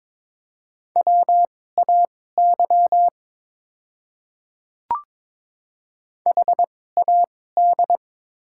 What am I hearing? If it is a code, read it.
Morse code, heard as a keyed tone, about 22 wpm, spaced as for 16 wpm, 700 Hz. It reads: WAY HAD
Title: 2x Character Spacing - Top 100 Words - 22wpm